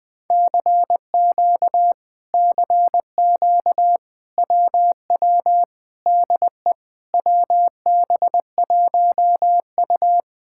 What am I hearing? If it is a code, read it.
Morse code, heard as a keyed tone, about 20 wpm, 700 Hz. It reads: CQ CQ WW DE WB1U
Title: Speed-Racing Common QSO Phrases - 20wpm